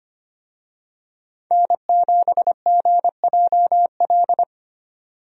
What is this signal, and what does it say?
Morse code, heard as a keyed tone, about 25 wpm, 700 Hz. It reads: N7GJL
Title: USA Call Signs - 5 Characters - 25wpm